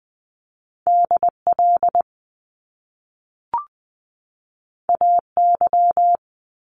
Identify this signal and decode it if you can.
Morse code, heard as a keyed tone, about 20 wpm, 700 Hz. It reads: D L A Y
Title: Top Two Letter Combinations in Words 20wpm